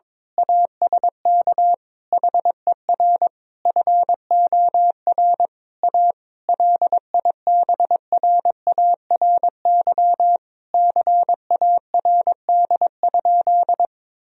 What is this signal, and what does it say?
Morse code, heard as a keyed tone, about 22 wpm, 700 Hz. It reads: ASK HER FOR A LIBRARY CARD?
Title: Sentences from Top 1000 Words - Review Part 4 - 22wpm